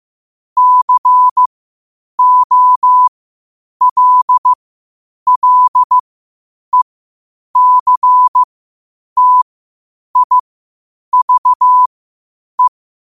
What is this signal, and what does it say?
Morse code, heard as a keyed tone, about 15 wpm, spaced as for 8 wpm, 1000 Hz. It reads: COLLECTIVE
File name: qotdfiles / 8wpm